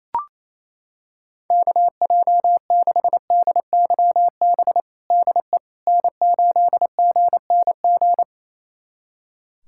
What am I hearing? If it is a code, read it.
Morse code, heard as a keyed tone, about 27 wpm, 700 Hz. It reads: KJ6DYB DE N8GNG
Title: USA Callsign DE Callsign 28wpm